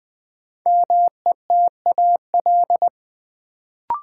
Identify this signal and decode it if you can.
Morse code, heard as a keyed tone, about 20 wpm, 700 Hz. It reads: METAL
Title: Sentences from Top 1000 Words - Review Part 6 - New Word Drill - 20wpm